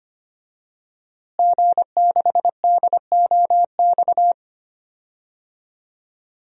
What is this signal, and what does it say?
Morse code, heard as a keyed tone, about 25 wpm, 700 Hz. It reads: G6DOX
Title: DXCC-100 Call Signs - 5 Characters - Repeated 3x - 25wpm